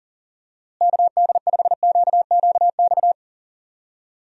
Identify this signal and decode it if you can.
Morse code, heard as a keyed tone, about 40 wpm, 700 Hz. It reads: KD5QQX
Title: USA Call Signs - 6 Characters - Repeated 2x - 40wpm